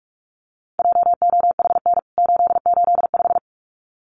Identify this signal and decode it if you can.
Morse code, heard as a keyed tone, about 44 wpm, 700 Hz. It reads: JOHN 885
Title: CWT Exchange 45wpm